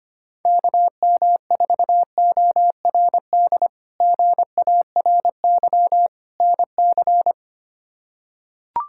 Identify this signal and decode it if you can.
Morse code, heard as a keyed tone, about 25 wpm, 700 Hz. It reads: KM4ORD GARY NC
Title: K1USN SST QSO Practice - 25wpm